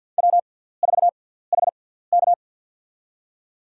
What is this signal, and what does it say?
Morse code, heard as a keyed tone, about 52 wpm, spaced as for 17 wpm, 700 Hz. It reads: W4HK
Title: USA Call Signs - 4 Characters - 6x Character Spacing - 50wpm